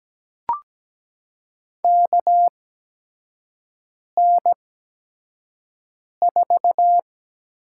Morse code, heard as a keyed tone, about 17 wpm, spaced as for 5 wpm, 700 Hz. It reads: KN4
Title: USA Call Signs - 6 Characters - 8x Character Spacing - 17wpm